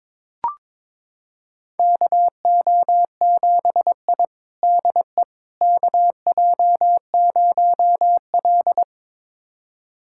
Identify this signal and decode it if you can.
Morse code, heard as a keyed tone, about 22 wpm, 700 Hz. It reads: KO7I DE KJ0L
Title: USA Callsign DE Callsign - 4 Characters - 22wpm